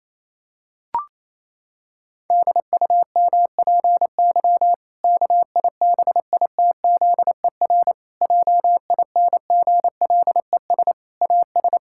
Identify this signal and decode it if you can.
Morse code, heard as a keyed tone, about 28 wpm, 700 Hz. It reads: DUMPY KIBITZER JINGLES AS
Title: Pangram Challenge v2 - 28wpm